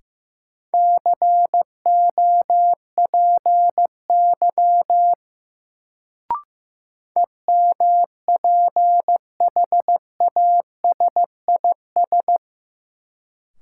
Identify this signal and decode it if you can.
Morse code, heard as a keyed tone, about 15 wpm, 700 Hz. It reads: COPY EMPHASIS
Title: Sentences from Top 1400 Words - New Word Drill - 15wpm